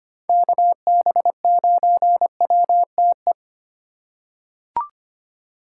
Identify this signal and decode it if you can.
Morse code, heard as a keyed tone, about 25 wpm, 700 Hz. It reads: KB9WTE
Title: USA Call Signs - 6 Characters - 3x Character Spacing - 25wpm